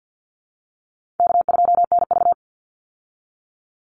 Morse code, heard as a keyed tone, about 49 wpm, 700 Hz. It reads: K2NV
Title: Call Signs Repeated 3x - 50wpm